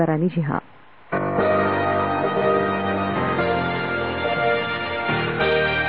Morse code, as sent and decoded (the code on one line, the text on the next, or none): none